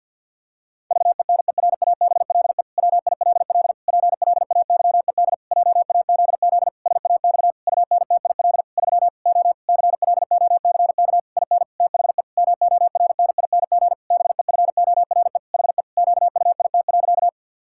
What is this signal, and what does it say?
Morse code, heard as a keyed tone, about 46 wpm, 700 Hz. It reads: VENERABLE WILL PLAYED JAZZ SAX UNTIL 3 O CLOCK IN THE MORNING BEFORE HE QUIT.
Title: Pangram Challenge v2 - 50wpm